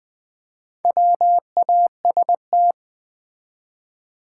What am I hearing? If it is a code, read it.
Morse code, heard as a keyed tone, about 20 wpm, 700 Hz. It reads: WAST
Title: Build A Word - 1 Letter At A Time - 3 to 5-Letter Words - 20wpm